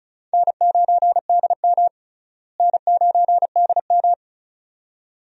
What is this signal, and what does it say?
Morse code, heard as a keyed tone, about 36 wpm, 700 Hz. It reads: N9DM N9DM
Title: USA Call Signs - 4 Characters - Repeated 2x - 35wpm